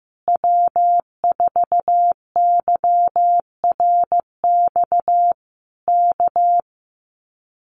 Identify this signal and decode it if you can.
Morse code, heard as a keyed tone, about 14 wpm, 700 Hz. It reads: W4YRX K
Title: Common QSO Phrases Encore 15wpm